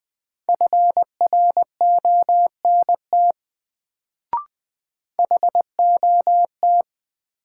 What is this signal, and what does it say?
Morse code, heard as a keyed tone, about 20 wpm, 700 Hz. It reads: FRONT HOT
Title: Sentences from Top 500 Words - New Word Drill - 20wpm